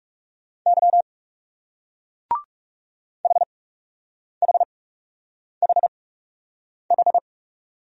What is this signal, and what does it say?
Morse code, heard as a keyed tone, about 37 wpm, spaced as for 8 wpm, 700 Hz. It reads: Y HHHH
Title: Speed-Racing Single Letters 30wpm